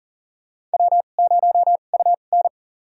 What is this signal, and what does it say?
Morse code, heard as a keyed tone, about 40 wpm, spaced as for 29 wpm, 700 Hz. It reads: W0UN
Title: USA Call Signs - 4 Characters - 2x Character Spacing - 40wpm